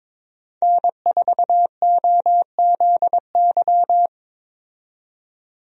Morse code, heard as a keyed tone, about 22 wpm, 700 Hz. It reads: N4OZY
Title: USA Call Signs - 5 Characters - 22wpm